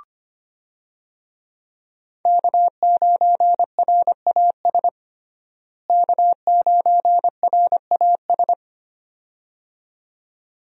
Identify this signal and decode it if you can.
Morse code, heard as a keyed tone, about 25 wpm, 700 Hz. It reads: K9RAS K9RAS
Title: USA Call Signs - 5 Characters - Repeated 2x - 25wpm